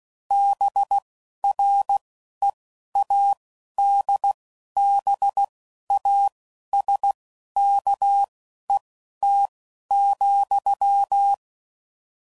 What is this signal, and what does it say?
Morse code, heard as a keyed tone, about 16 wpm, spaced as for 11 wpm, 800 Hz. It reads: BREADBASKET,